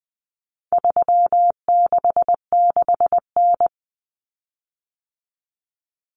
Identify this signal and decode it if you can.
Morse code, heard as a keyed tone, about 20 wpm, 700 Hz. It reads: 366N
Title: Common Cut Numbers - 4-Digit 20wpm